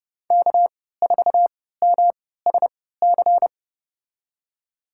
Morse code, heard as a keyed tone, about 32 wpm, spaced as for 17 wpm, 700 Hz. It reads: K4MSC
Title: USA Call Signs - 5 Characters - 3x Character Spacing - 30wpm